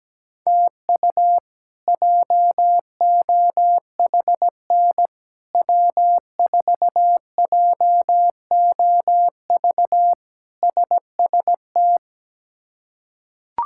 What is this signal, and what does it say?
Morse code, heard as a keyed tone, about 17 wpm, 700 Hz. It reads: TU JOHN W4JOV SST
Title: K1USN SST QSO Practice - 17wpm